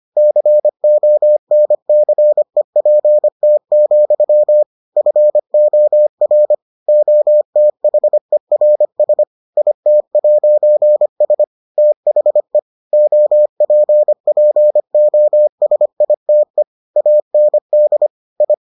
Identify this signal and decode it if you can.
Morse code, heard as a keyed tone, about 25 wpm, 600 Hz. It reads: CONCEPT, FOR OTHERS IT'S THE OPPOSITE AND I